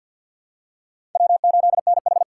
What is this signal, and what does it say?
Morse code, heard as a keyed tone, about 53 wpm, 700 Hz. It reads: W8NH
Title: USA Call Signs - 4 Characters - 6x Character Spacing - 50wpm